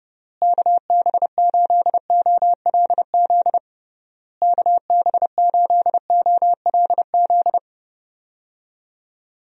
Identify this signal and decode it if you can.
Morse code, heard as a keyed tone, about 30 wpm, 700 Hz. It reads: KB8OLZ KB8OLZ
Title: USA Call Signs - 6 Characters - Repeated 3x - 30wpm